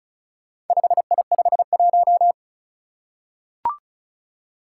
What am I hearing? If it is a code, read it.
Morse code, heard as a keyed tone, about 36 wpm, 690 Hz. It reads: 5I51
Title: Four Challenging Letter-Number Combinations 35wpm